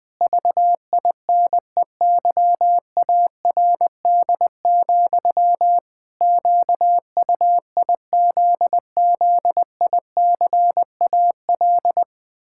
Morse code, heard as a keyed tone, about 21 wpm, 700 Hz. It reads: VINEYARD, QUIZZICAL
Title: Pangram Challenge v2 - 20wpm